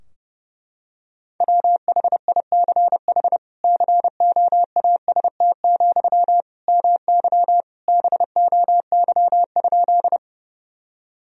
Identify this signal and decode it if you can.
Morse code, heard as a keyed tone, about 30 wpm, 700 Hz. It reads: WHICH COAST, MY BOY?
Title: Sentences from Top 1000 Words - Review Part 4 - 30wpm